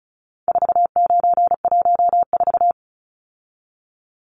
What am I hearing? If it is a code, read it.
Morse code, heard as a keyed tone, about 34 wpm, 700 Hz. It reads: 4914
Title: Numbers - 4-Digit 35wpm